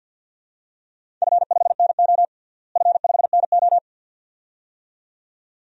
Morse code, heard as a keyed tone, about 48 wpm, 700 Hz. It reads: F5NO F5NO
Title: DXCC-100 Call Signs - 4 Characters - Repeated 2x - 50wpm